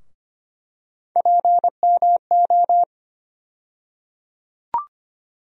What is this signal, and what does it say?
Morse code, heard as a keyed tone, about 25 wpm, 700 Hz. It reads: PMO
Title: USA Call Sign Postfixes - 3 Letters 25wpm